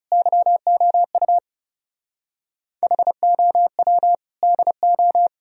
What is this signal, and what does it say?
Morse code, heard as a keyed tone, about 36 wpm, 700 Hz. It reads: YOU HOW DO
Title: Speed-Racing - Sets of 3 Words 30wpm